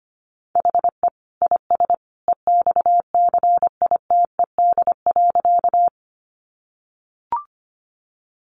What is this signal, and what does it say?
Morse code, heard as a keyed tone, about 25 wpm, 700 Hz. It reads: HE IS EXCITED.
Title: Additional Easy Phrases 25wpm